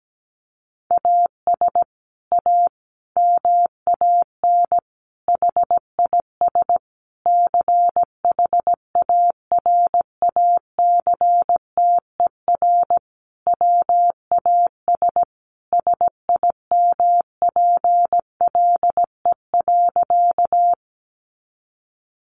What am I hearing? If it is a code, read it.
Morse code, heard as a keyed tone, about 17 wpm, 700 Hz. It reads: AS A MAN HIS CHARACTER WAS SIMPLE.
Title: Sentences from Top 1200 Words 17wpm